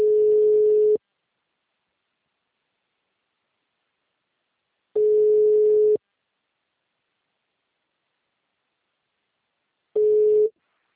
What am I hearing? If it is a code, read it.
Ringback tone (Europe, 425 Hz).